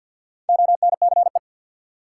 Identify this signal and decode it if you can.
Morse code, heard as a keyed tone, about 53 wpm, 700 Hz. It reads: ONCE
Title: Four Letter Words - 50wpm